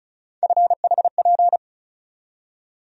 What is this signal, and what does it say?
Morse code, heard as a keyed tone, about 34 wpm, 700 Hz. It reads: FHP
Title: USA Call Sign Postfixes - 3 Letters 35wpm